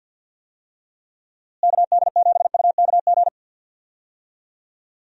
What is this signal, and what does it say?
Morse code, heard as a keyed tone, about 52 wpm, 700 Hz. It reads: KD7UKG
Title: USA Call Signs - 6 Characters - Repeated 2x - 50wpm